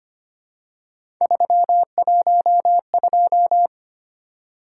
Morse code, heard as a keyed tone, about 25 wpm, 700 Hz. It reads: 312
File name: Three Letter-Number Combinations - 25wpm